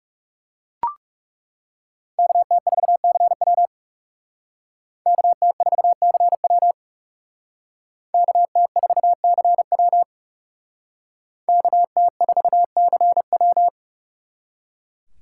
Morse code, heard as a keyed tone, about 38 wpm, 700 Hz. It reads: KT4CW KT4CW KT4CW KT4CW